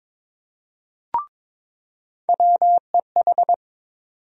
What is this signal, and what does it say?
Morse code, heard as a keyed tone, about 22 wpm, 700 Hz. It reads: WEH
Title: USA Call Sign Postfixes - 3 Letters 22wpm